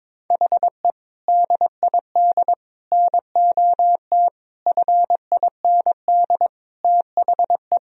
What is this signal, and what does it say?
Morse code, heard as a keyed tone, about 22 wpm, 700 Hz. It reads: HE DID NOT FIND THE
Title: Sentences from Top 1000 Words - Review Part 3 - 22wpm